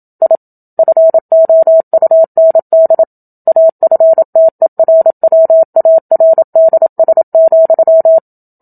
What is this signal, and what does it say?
Morse code, heard as a keyed tone, about 27 wpm, 650 Hz. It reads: I FOUND AFTERWARDS,
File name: Book Two / 01-Chapter 1 - 27 WPM